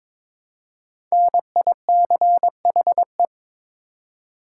Morse code, heard as a keyed tone, about 22 wpm, 690 Hz. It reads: NICHE
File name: 5 Letter Words - First 3 Letters Given - 22wpm